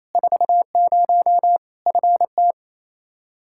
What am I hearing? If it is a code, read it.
Morse code, heard as a keyed tone, about 28 wpm, 700 Hz. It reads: 40 FT
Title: K1USN SST QSO Practice 28wpm